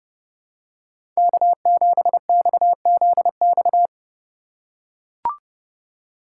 Morse code, heard as a keyed tone, about 30 wpm, 700 Hz. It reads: K7XZX
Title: USA Call Signs - 4x Character Spacing - 30wpm